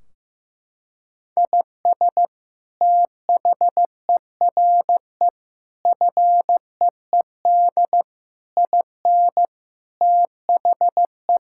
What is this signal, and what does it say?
Morse code, heard as a keyed tone, about 16 wpm, 700 Hz. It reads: IS THERE FEED IN THE
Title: Sentences from Top 1600 Words 15wpm